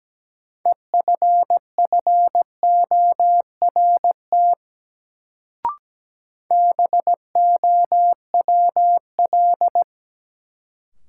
Morse code, heard as a keyed tone, about 17 wpm, 700 Hz. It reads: EFFORT BOWL
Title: Sentences from Top 1200 Words - New Word Drill Alt - 17wpm